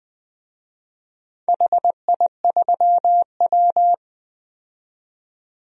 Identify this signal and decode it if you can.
Morse code, heard as a keyed tone, about 20 wpm, 700 Hz. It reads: HI3W
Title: DXCC-100 Call Signs - 4 Characters - 20wpm